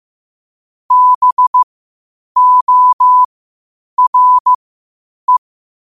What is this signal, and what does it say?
Morse code, heard as a keyed tone, about 15 wpm, spaced as for 8 wpm, 1000 Hz. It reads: BORE
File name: qotdfiles / 8wpm